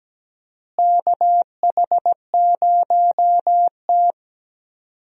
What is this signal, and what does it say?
Morse code, heard as a keyed tone, about 17 wpm, 700 Hz. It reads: KH0T
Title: USA Call Signs - 4 Characters - Repeated 3x - 17wpm